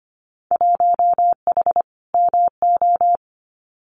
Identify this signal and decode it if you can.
Morse code, heard as a keyed tone, about 25 wpm, 700 Hz. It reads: 1H MO
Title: Winter Field Day Exchanges 25wpm